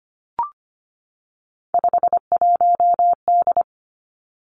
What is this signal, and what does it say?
Morse code, heard as a keyed tone, about 25 wpm, 700 Hz. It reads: 51D
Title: Three Challenging Letter-Number Combinations 25wpm